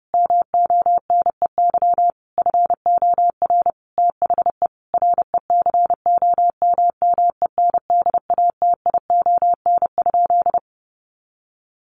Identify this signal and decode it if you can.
Morse code, heard as a keyed tone, about 30 wpm, 700 Hz. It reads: MONEY FOR THE RECOMMENDATION?
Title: Sentences from Top 1400 Words 30wpm